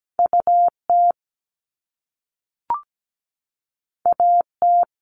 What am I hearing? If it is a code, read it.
Morse code, heard as a keyed tone, about 17 wpm, 700 Hz. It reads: UT AT
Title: Top Two Letter Combinations in Words 17wpm